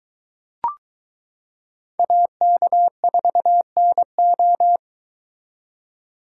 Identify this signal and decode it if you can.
Morse code, heard as a keyed tone, about 23 wpm, 700 Hz. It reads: AK4NO